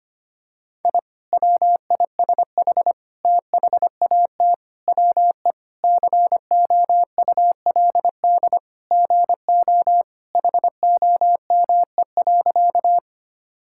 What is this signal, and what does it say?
Morse code, heard as a keyed tone, about 25 wpm, 700 Hz. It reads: I WISH THAT WE COULD GO HOME.
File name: practice / 25wpm / Sentences with 2-Word Phrases - 25wpm